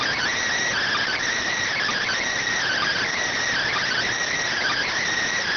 Emergency Alert System (SAME) tones (header bursts), heard as harsh buzzing bursts. It is heard from 0.7 to 4.9 s.
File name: findings / CT SAGITAL 27.gen-18 rx